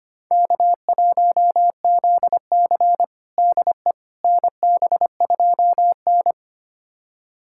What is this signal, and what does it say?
Morse code, heard as a keyed tone, about 25 wpm, 700 Hz. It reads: K1ZC DE NB2N
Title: USA Callsign DE Callsign - 4 Characters - 25wpm